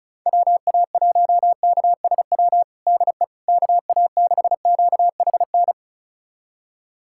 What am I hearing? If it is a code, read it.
Morse code, heard as a keyed tone, about 34 wpm, 700 Hz. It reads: WA1KSW DE KA6QHN
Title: USA Callsign DE Callsign - 6 Characters - 35wpm